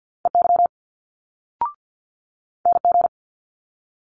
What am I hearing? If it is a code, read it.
Morse code, heard as a keyed tone, about 50 wpm, 700 Hz. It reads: EY NG
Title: Multi-letter Phonemes Quick Fire 50wpm